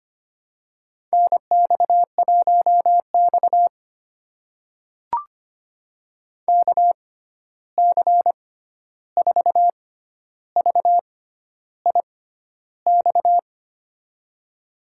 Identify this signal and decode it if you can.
Morse code, heard as a keyed tone, about 25 wpm, spaced as for 11 wpm, 700 Hz. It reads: NX1X KC4VIX
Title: USA Call Signs - 6x Character Spacing - 25wpm